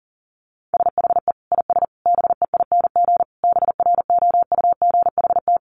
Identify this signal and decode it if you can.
Morse code, heard as a keyed tone, about 40 wpm, 700 Hz. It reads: SHE IS BEING BROUGHT
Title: Sentences from Top 600 Words 40wpm